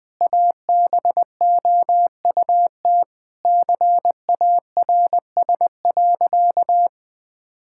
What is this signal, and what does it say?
Morse code, heard as a keyed tone, about 20 wpm, 700 Hz. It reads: ABOUT CARS.
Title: Sentences with 5-Word Phrases - Spoken Beginning - 20wpm